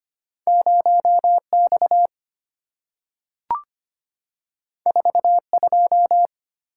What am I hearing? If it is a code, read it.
Morse code, heard as a keyed tone, about 25 wpm, 690 Hz. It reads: 0X 42
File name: Two Letter-Number Combinations - 25wpm